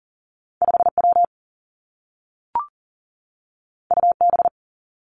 Morse code, heard as a keyed tone, about 42 wpm, 700 Hz. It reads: 5 W U B
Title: Two Challenging Letter-Number Combinations 40wpm